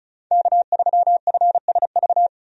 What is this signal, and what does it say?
Morse code, heard as a keyed tone, about 34 wpm, 700 Hz. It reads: K3FSV
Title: USA Call Signs - 5 Characters - 35wpm